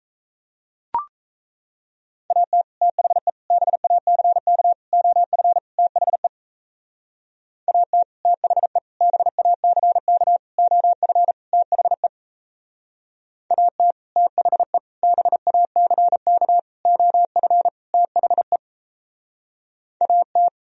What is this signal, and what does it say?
Morse code, heard as a keyed tone, about 33 wpm, 700 Hz. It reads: AT THE BACK OF THE AT THE BACK OF THE AT THE BACK OF THE AT
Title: Speed-Racing - Sets of 5 Words 28wpm